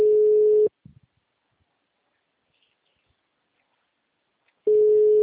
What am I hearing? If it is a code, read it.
Ringback tone (Europe, 425 Hz).